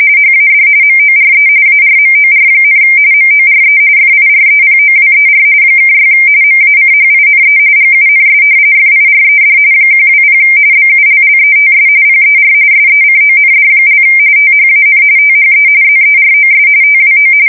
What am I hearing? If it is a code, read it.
RTTY, about 45.45 baud. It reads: HAD MORE ARMOR, MORE FIREPOWER (6 MACHINE GUNS FIRED 120 BULLETS PER SECOND) THAN THE ZEROS TWO
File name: fsk170 / usnavy4